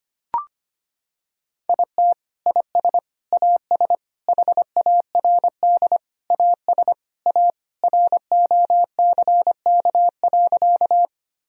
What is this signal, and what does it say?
Morse code, heard as a keyed tone, about 25 wpm, 700 Hz. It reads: IT IS AS HARD AS A ROCK.
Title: Sentences from Top 1000 Words - Review Part 2 - 25wpm